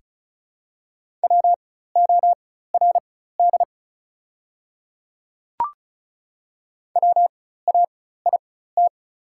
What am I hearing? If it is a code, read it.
Morse code, heard as a keyed tone, about 36 wpm, spaced as for 16 wpm, 690 Hz. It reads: WORD WAIT
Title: 4x Character Spacing - Top 500 Words - 35wpm